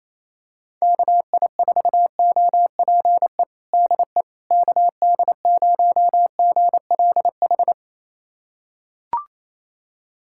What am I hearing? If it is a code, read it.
Morse code, heard as a keyed tone, about 28 wpm, 700 Hz. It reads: KI4OPE DE KD0GLH
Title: USA Callsign DE Callsign 28wpm